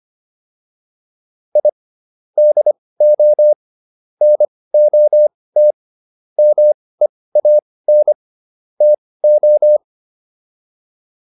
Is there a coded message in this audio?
Yes: Morse code from 1.6 to 9.8 s, heard as a keyed tone, about 25 wpm, spaced as for 18 wpm, 600 Hz. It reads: I DO NOT MEAN TO